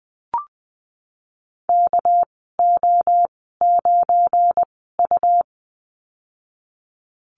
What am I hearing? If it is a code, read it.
Morse code, heard as a keyed tone, about 20 wpm, spaced as for 15 wpm, 700 Hz. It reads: KO9U
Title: 200 Lesson 32g - Letter k - Call Signs 20wpm